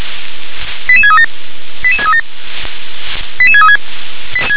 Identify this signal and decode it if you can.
Five-tone selective call (ZVEI), heard as quick runs of five tones, tones of about 70 ms. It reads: 88637 88637 88636